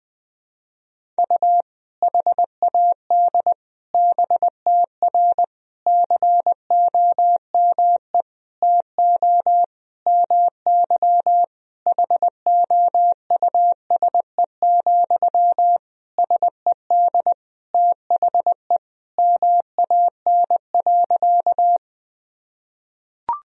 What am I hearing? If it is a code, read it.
Morse code, heard as a keyed tone, about 20 wpm, 700 Hz. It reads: U HAD BTR COME TO MY HOUSE, SED THE MAN.
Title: Sentences from Top 400 Words and Abbreviations 20wpm